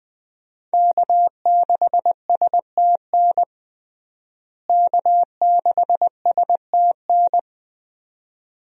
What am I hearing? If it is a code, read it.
Morse code, heard as a keyed tone, about 20 wpm, 700 Hz. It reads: K6STN K6STN
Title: USA Call Signs - 5 Characters - Repeated 2x - 20wpm